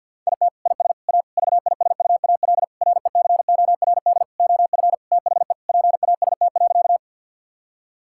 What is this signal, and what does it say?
Morse code, heard as a keyed tone, about 50 wpm, 700 Hz. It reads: IT IS A VISUAL RECORD OF THE PAST.
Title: Sentences from Top 1000 Words - Review Part 6 - 50wpm